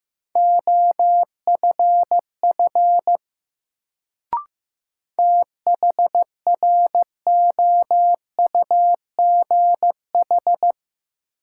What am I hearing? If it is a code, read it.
Morse code, heard as a keyed tone, about 15 wpm, 700 Hz. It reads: OFF THROUGH
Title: Sentences from Top 200 Words - New Word Drill - 15wpm